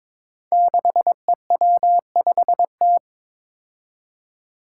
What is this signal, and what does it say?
Morse code, heard as a keyed tone, about 22 wpm, 700 Hz. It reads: 6EW5T
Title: Five Challenging Letter-Number Combinations 22wpm